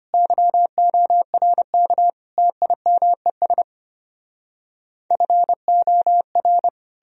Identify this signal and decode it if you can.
Morse code, heard as a keyed tone, about 33 wpm, 700 Hz. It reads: YORK TIMES FOR